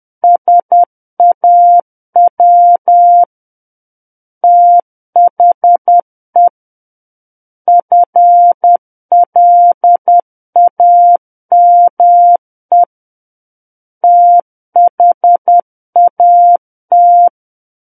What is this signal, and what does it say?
Morse code, heard as a keyed tone, about 10 wpm, 700 Hz. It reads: SAW THE FLAME THAT
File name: Book One / 01-Chapter 1 - 10 WPM